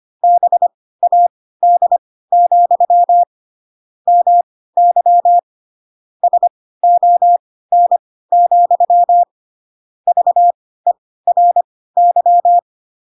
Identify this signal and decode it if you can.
Morse code, heard as a keyed tone, about 25 wpm, spaced as for 16 wpm, 700 Hz. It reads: BAD, MY SON, VERY
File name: Pinocchio / Pinocchio ch3 10 25 ik1hge